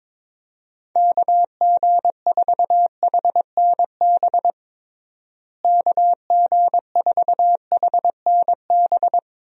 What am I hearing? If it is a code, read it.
Morse code, heard as a keyed tone, about 21 wpm, 700 Hz. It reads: KG4HNB KG4HNB
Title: USA Call Signs - 6 Characters - Repeated 3x - 22wpm